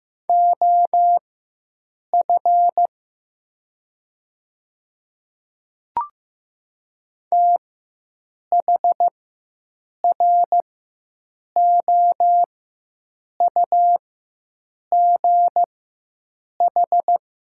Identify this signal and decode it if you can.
Morse code, heard as a keyed tone, about 15 wpm, spaced as for 7 wpm, 700 Hz. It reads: OF THROUGH
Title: 4x Character Spacing - Top 200 Words - 15wpm